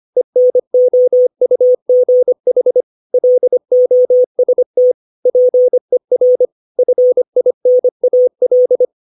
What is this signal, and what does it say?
Morse code, heard as a keyed tone, about 25 wpm, 500 Hz. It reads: ENOUGH LOST PER FINAL
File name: 632 words / 632 Words 25WPM